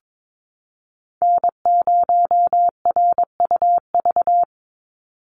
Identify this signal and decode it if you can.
Morse code, heard as a keyed tone, about 22 wpm, 700 Hz. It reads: N0RUV
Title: USA Call Signs 22wpm